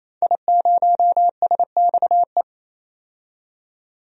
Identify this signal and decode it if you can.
Morse code, heard as a keyed tone, about 28 wpm, 700 Hz. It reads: I0SXE
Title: DXCC-100 Call Signs - 5 Characters - 28wpm